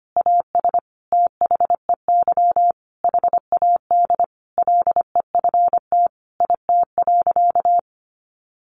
Morse code, heard as a keyed tone, about 25 wpm, 700 Hz. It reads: AS THEY HAD LEFT IT.
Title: A Sound Of Thunder 25wpm